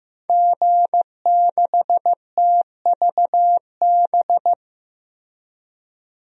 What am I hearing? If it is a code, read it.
Morse code, heard as a keyed tone, about 15 wpm, 700 Hz. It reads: G6TVB
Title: DXCC-100 Call Signs - 5 Characters - Repeated 3x - 15wpm